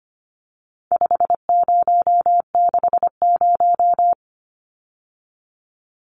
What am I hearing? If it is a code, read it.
Morse code, heard as a keyed tone, about 25 wpm, 700 Hz. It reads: 5060
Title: Numbers - 4-Digit 25wpm